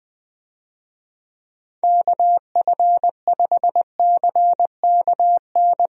Morse code, heard as a keyed tone, about 20 wpm, 700 Hz. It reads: KF5CKN
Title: USA Call Signs - 6 Characters - Repeated 2x - 20wpm